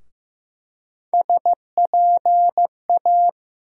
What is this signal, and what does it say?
Morse code, heard as a keyed tone, about 15 wpm, 700 Hz. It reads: SPA
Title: USA Call Sign Postfixes - 3 Letters 15wpm